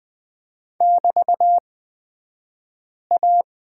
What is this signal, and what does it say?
Morse code, heard as a keyed tone, about 20 wpm, spaced as for 5 wpm, 700 Hz. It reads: <BT>A
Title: 254 Lesson 41d - Period - Two Characters 20wpm